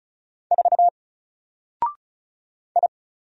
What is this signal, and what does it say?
Morse code, heard as a keyed tone, about 33 wpm, 700 Hz. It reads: 4 I
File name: Single-Letter-Number-35wpm